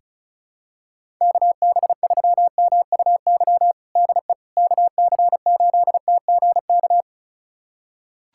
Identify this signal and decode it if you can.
Morse code, heard as a keyed tone, about 34 wpm, 690 Hz. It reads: KB3MUY DE KC8TGK
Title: USA Callsign DE Callsign - 6 Characters - 35wpm